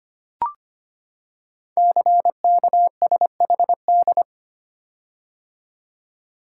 Morse code, heard as a keyed tone, about 25 wpm, 700 Hz. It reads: CKSHD
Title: Code Groups - 5 Letters - 25wpm